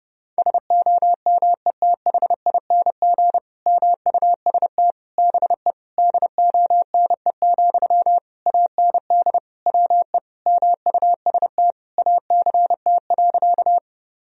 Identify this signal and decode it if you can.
Morse code, heard as a keyed tone, about 32 wpm, 700 Hz. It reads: SOMETHING MUST BE DONE, AND WE MUST ACT.
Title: Sentences from Top 1000 Words - Review Part 1 - 30wpm